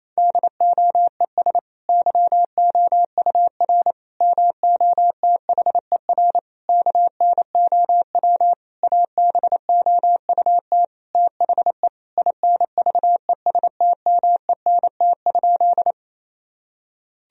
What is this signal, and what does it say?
Morse code, heard as a keyed tone, about 28 wpm, 700 Hz. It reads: DOES YOUR MOTHER KNOW ABOUT THE INVESTMENT?
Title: Sentences from Top 1000 Words - Review Part 4 - 28wpm